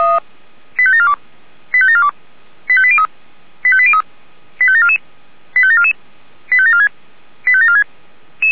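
Five-tone selective call (ZVEI), heard as quick runs of five tones, tones of about 70 ms. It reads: 85742 85742 85803 85803 85744 85744 85746 85746
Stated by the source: Five-tone ZVEI selective call 85719, received by SDR.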